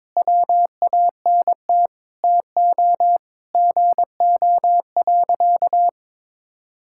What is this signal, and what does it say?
Morse code, heard as a keyed tone, about 22 wpm, 700 Hz. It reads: WANT TO GO.
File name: practice / 22wpm / Sentences with 3-Word Phrases - 22wpm